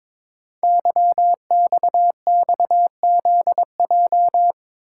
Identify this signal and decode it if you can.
Morse code, heard as a keyed tone, about 22 wpm, 700 Hz. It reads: YXXZJ
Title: Code Groups - 5 Letters - 22wpm